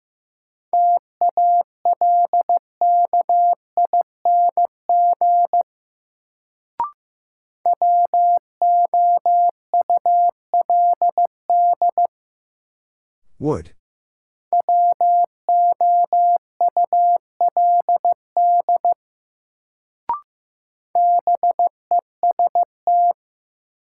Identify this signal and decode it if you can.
Morse code, heard as a keyed tone, about 15 wpm, 700 Hz. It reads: TALKING WOULD WOULD BEST
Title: Sets of 2 Words - New Word Drill - 15wpm